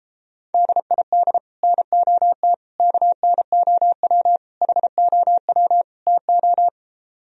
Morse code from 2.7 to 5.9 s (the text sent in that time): KNOW HOW